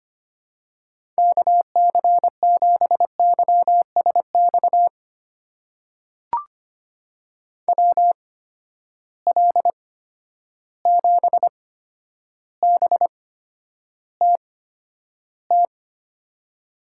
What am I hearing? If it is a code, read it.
Morse code, heard as a keyed tone, about 25 wpm, spaced as for 11 wpm, 700 Hz. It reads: KC7YSX WL7BTT
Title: USA Call Signs - 6 Characters - 8x Character Spacing - 25wpm